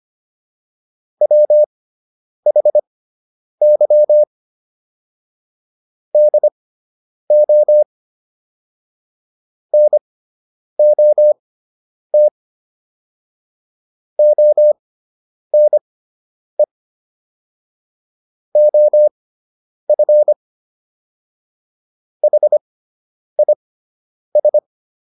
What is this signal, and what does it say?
Morse code, heard as a keyed tone, about 25 wpm, spaced as for 9 wpm, 600 Hz. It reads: WHY DO NOT ONE OF HIS